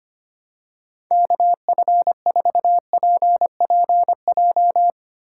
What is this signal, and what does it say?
Morse code, heard as a keyed tone, about 25 wpm, 700 Hz. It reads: KF4PPJ
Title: USA Call Signs - 4x Character Spacing - 25wpm